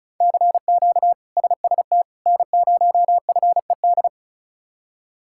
Morse code, heard as a keyed tone, about 34 wpm, 700 Hz. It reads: CQ SST N0FED
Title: K1USN SST QSO Practice - 35wpm